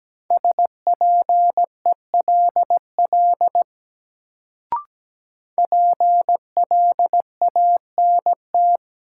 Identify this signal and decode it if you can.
Morse code, heard as a keyed tone, about 17 wpm, 700 Hz. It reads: SPELL PLANT
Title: Sentences from Top 1000 Words - Review Part 1 - New Word Drill - 17wpm